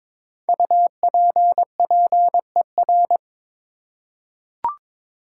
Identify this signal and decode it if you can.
Morse code, heard as a keyed tone, about 22 wpm, 700 Hz. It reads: UPPER
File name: 5 Letter Words - First 3 Letters Given - 22wpm